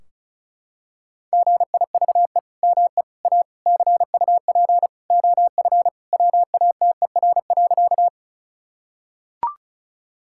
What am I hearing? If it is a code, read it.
Morse code, heard as a keyed tone, about 36 wpm, 700 Hz. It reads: GIVE ME A CUP OF WATER.
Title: Sentences from Top 1600 Words 35wpm